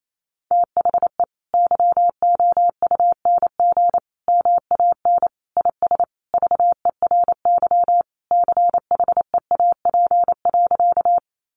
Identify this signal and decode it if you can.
Morse code, heard as a keyed tone, about 28 wpm, 700 Hz. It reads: THE YOUNG MAN IS VERY CHEAP.
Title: Sentences from Top 1200 Words 28wpm